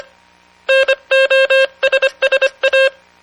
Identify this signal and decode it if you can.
Morse code, heard as a keyed tone, about 25 wpm, 540 Hz. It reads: NOSSA